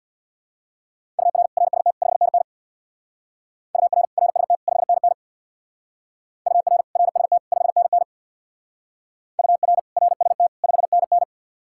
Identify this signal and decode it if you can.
Morse code, heard as a keyed tone, about 62 wpm, 700 Hz. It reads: UR RST 5NN UR RST 5NN UR RST 5NN UR RST 5NN
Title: Speed-Racing Common QSO Phrases - 50wpm